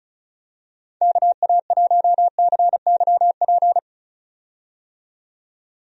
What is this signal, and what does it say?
Morse code, heard as a keyed tone, about 36 wpm, 700 Hz. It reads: KA1CYP
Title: USA Call Signs - 6 Characters - 35wpm